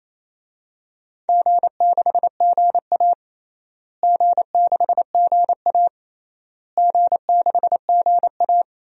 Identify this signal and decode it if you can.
Morse code, heard as a keyed tone, about 28 wpm, 700 Hz. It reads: G6GA G6GA G6GA
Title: DXCC-100 Call Signs - 4 Characters - Repeated 3x - 28wpm